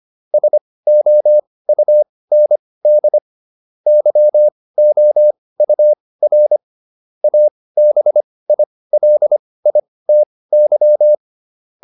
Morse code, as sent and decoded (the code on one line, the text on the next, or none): ... --- ..- -. -.. / -.-- --- ..- .-. / .- -... .. .-.. .. - -.--
SOUND YOUR ABILITY